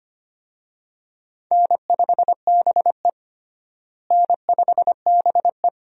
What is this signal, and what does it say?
Morse code, heard as a keyed tone, about 25 wpm, 700 Hz. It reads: N5BE N5BE
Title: USA Call Signs - 4 Characters - Repeated 2x - 25wpm